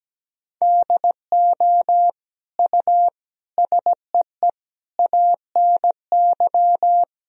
Morse code, heard as a keyed tone, about 17 wpm, 700 Hz. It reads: DO U SEE ANY
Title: Sentences from Top 500 Words and Abbreviations 17wpm